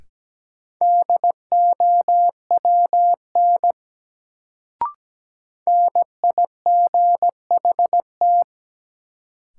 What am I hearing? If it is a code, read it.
Morse code, heard as a keyed tone, about 17 wpm, 700 Hz. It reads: DOWN NIGHT
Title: Sentences from Top 300 Words - New Word Drill - 17wpm